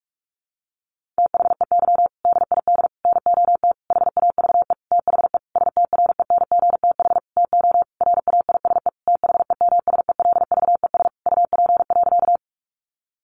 Morse code, heard as a keyed tone, about 44 wpm, 700 Hz. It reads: THEY DID NOT HAVE THE STRENGTH TO RAISE THEMSELVES UP.
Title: Sentences from Top 1400 Words 45wpm